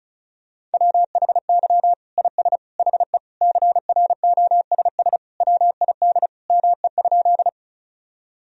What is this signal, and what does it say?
Morse code, heard as a keyed tone, about 36 wpm, 700 Hz. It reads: WHY IS HE CROSS WID ME?
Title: Sentences from Top 300 Words and Abbreviations 35wpm